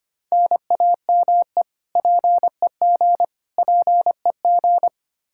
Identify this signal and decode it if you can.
Morse code, heard as a keyed tone, about 25 wpm, 700 Hz. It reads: NAME PEG PEG
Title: Practice QSO - Exchange 1 - Beginning - 25wpm